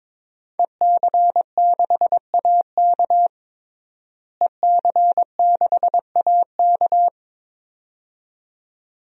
Morse code, heard as a keyed tone, about 22 wpm, 700 Hz. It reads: EC6AK EC6AK
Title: DXCC-100 Call Signs - 5 Characters - Repeated 2x - 22wpm